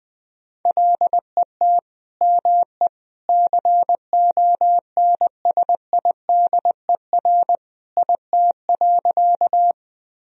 Morse code, heard as a keyed tone, about 20 wpm, 700 Hz. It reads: LET ME CONSIDER IT.
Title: Sentences from Top 1000 Words - Review Part 4 - 20wpm